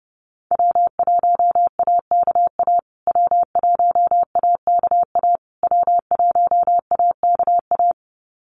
Morse code, heard as a keyed tone, about 30 wpm, 700 Hz. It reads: W1AKA W1AKA W1AKA
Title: USA Call Signs Repeated 3x 30wpm